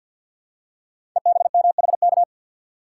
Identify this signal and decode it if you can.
Morse code, heard as a keyed tone, about 51 wpm, 700 Hz. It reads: EBMHK
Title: Code Groups - 5 Letters - 50wpm